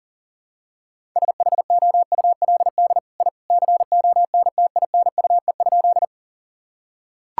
Morse code, heard as a keyed tone, about 37 wpm, 700 Hz. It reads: SHOULD I CONTINUE?
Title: Sentences from Top 1000 Words - Review Part 6 - 40wpm